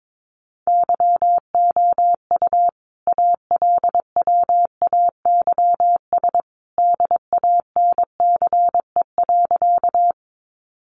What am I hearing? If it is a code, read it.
Morse code, heard as a keyed tone, about 22 wpm, 700 Hz. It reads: YOU ALWAYS DANCE.
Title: Sentences from Top 700 Words 22wpm